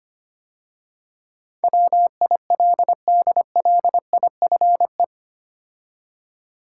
Morse code, heard as a keyed tone, about 25 wpm, 690 Hz. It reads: WILDLIFE
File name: 8 Letter Words - First 3 Letters Given - 25wpm